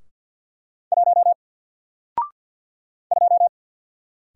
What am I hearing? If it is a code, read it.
Morse code, heard as a keyed tone, about 50 wpm, 700 Hz. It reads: 1 2
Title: Single Letter-Number 50wpm